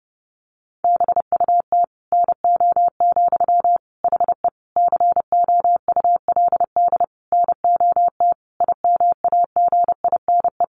Morse code, heard as a keyed tone, about 30 wpm, 700 Hz. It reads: BUT NO, HE COULD NOT IMAGINE
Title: Sentences from Top 1400 Words 30wpm